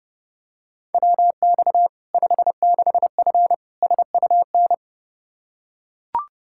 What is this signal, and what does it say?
Morse code, heard as a keyed tone, about 30 wpm, 700 Hz. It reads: WX 56F SUN
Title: Practice QSO - Exchange 2 - Info - 30wpm